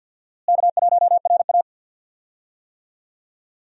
Morse code, heard as a keyed tone, about 52 wpm, 700 Hz. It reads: K1RA
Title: USA Call Signs - 4 Characters - Repeated 2x - 50wpm